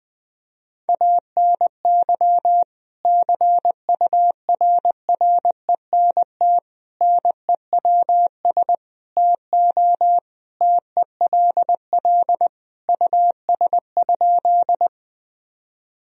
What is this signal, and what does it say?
Morse code, heard as a keyed tone, about 20 wpm, 700 Hz. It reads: ANY CURRENT NEWS TO TELL US?
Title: Sentences from Top 1000 Words - Review Part 4 - 20wpm